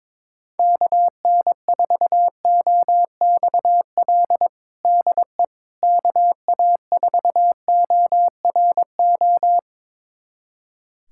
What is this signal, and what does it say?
Morse code, heard as a keyed tone, about 22 wpm, 700 Hz. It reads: KN4OXL DE KA4ORO
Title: USA Callsign DE Callsign 22wpm